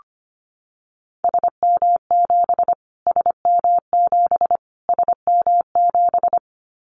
Morse code, heard as a keyed tone, about 25 wpm, 700 Hz. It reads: SM7 SM7 SM7
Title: Callsign Prefixes for Top 25 DXCC - Repeated 3x - 25wpm